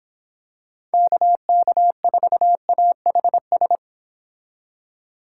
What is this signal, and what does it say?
Morse code, heard as a keyed tone, about 25 wpm, 700 Hz. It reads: KK4AHS